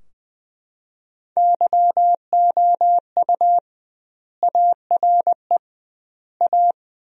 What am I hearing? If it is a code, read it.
Morse code, heard as a keyed tone, about 20 wpm, 700 Hz. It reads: YOU ARE A
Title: 2x Word-Spacing - Sentences from Top 400 Words - 20wpm